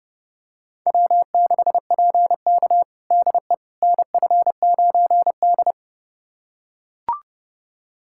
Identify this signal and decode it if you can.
Morse code, heard as a keyed tone, about 32 wpm, 700 Hz. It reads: W6PK DE NF9D
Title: USA Callsign DE Callsign - 4 Characters - 30wpm